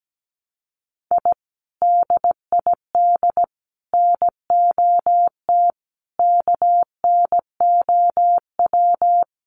Morse code, heard as a keyed tone, about 17 wpm, 700 Hz. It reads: I DID NOT KNOW
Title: Sets of 5 Words Encore2 17wpm